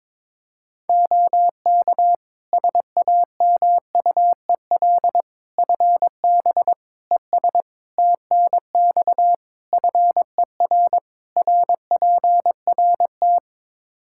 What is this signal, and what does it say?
Morse code, heard as a keyed tone, about 21 wpm, 700 Hz. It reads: OK SAMUEL FB ES TNX FER RPRT
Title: K1USN SST QSO Practice 22wpm